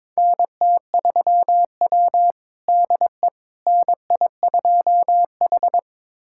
Morse code, heard as a keyed tone, about 22 wpm, 700 Hz. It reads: NT3W DE NI2H
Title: USA Callsign DE Callsign - 4 Characters - 22wpm